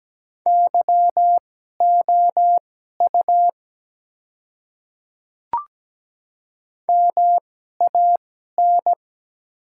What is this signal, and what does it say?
Morse code, heard as a keyed tone, about 17 wpm, spaced as for 12 wpm, 700 Hz. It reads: YOU MAN
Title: 2x Character Spacing - Top 200 Words - 17wpm